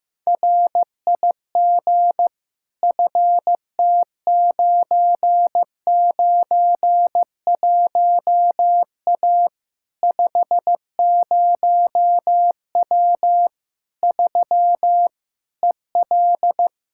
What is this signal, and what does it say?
Morse code, heard as a keyed tone, about 15 wpm, 700 Hz. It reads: RIG FT991A 50W 3 EL
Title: Practice QSO - Exchange 2 - Info - 15wpm